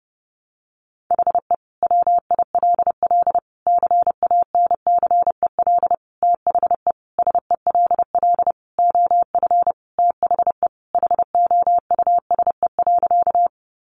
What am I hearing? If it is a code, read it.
Morse code, heard as a keyed tone, about 30 wpm, 700 Hz. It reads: HE WILL CANCEL THE SELL OF THE HOUSE.
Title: Sentences from Top 1200 Words 30wpm